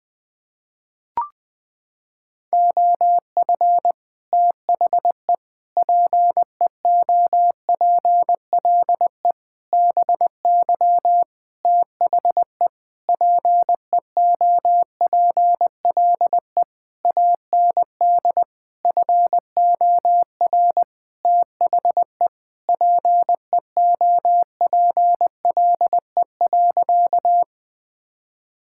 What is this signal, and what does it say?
Morse code, heard as a keyed tone, about 21 wpm, 700 Hz. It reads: OF THE PEOPLE BY THE PEOPLE AND FOR THE PEOPLE.
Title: Sentences with 8-Word Phrases - 20wpm